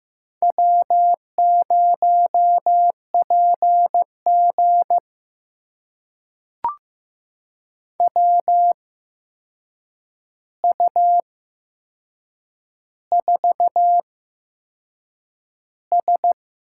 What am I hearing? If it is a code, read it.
Morse code, heard as a keyed tone, about 15 wpm, spaced as for 6 wpm, 700 Hz. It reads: W0PG WU4S
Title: USA Call Signs - 8x Character Spacing - 15wpm